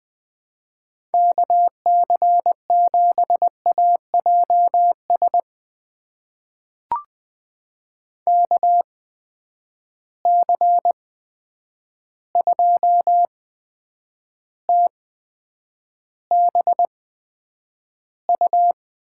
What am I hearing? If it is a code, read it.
Morse code, heard as a keyed tone, about 20 wpm, spaced as for 9 wpm, 700 Hz. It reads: KC7AJS KC2TBU
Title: USA Call Signs - 6 Characters - 8x Character Spacing - 20wpm